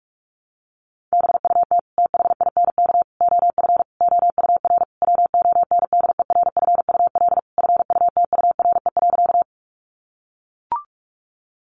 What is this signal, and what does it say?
Morse code, heard as a keyed tone, about 44 wpm, 700 Hz. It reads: BUT THINK OF OUR WONDERFUL FUTURE.
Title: Sentences from Top 1100 Words 45wpm